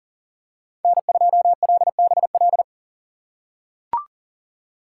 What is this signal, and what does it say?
Morse code, heard as a keyed tone, about 40 wpm, 700 Hz. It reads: N2LBL
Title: USA Call Signs - 5 Characters - 3x Character Spacing - 40wpm